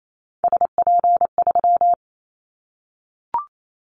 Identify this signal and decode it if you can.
Morse code, heard as a keyed tone, about 28 wpm, 700 Hz. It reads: SP3
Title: Callsign Prefixes for Top 25 DXCC - Repeated 3x - 28wpm